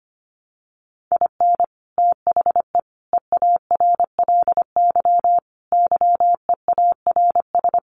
Morse code, heard as a keyed tone, about 25 wpm, 700 Hz. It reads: IN THE EARLY YEARS
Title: Sets of 5 Words Encore 25wpm